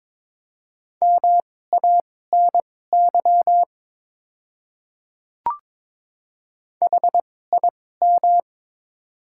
Morse code, heard as a keyed tone, about 21 wpm, spaced as for 16 wpm, 700 Hz. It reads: MANY HIM
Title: 2x Character Spacing - Top 300 Words - 22wpm